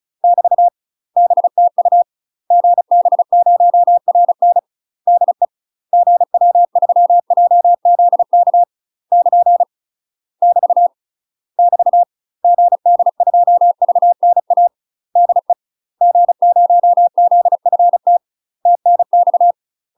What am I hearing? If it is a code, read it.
Morse code, heard as a keyed tone, about 36 wpm, 700 Hz. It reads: <BT> BTU GB0RN DE GW3JZK <KN> <BT> <BT> GD2VNA DE G0ZFT TNX
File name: UK MASTER QSOs TRAINING / 35WPM / UK MASTER QSOs 0023